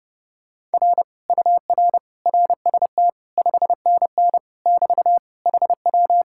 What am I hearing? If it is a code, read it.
Morse code, heard as a keyed tone, about 30 wpm, 700 Hz. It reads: R UR RST 5NN <BT> HW